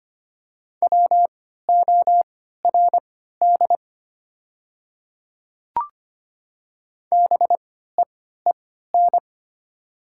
Morse code, heard as a keyed tone, about 25 wpm, spaced as for 14 wpm, 700 Hz. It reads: WORD BEEN
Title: 3x Character Spacing - Top 200 Words - 25wpm